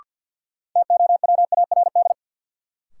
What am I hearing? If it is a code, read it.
Morse code, heard as a keyed tone, about 53 wpm, 700 Hz. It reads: TOWARD
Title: Sentences from Top 1300 Words - New Word Drill Alt - 50wpm